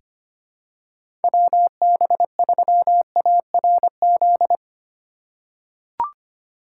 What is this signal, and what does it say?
Morse code, heard as a keyed tone, about 25 wpm, 700 Hz. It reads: WB3ARZ
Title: USA Call Signs - 6 Characters - 3x Character Spacing - 25wpm